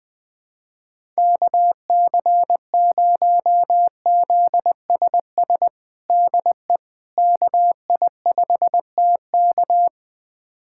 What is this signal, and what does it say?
Morse code, heard as a keyed tone, about 20 wpm, 700 Hz. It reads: KC0ZSS DE KI5TK
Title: USA Callsign DE Callsign 20wpm